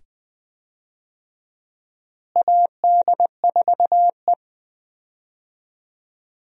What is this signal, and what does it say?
Morse code, heard as a keyed tone, about 20 wpm, 700 Hz. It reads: AD4E
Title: USA Call Signs - 4 Characters - Repeated 2x - 20wpm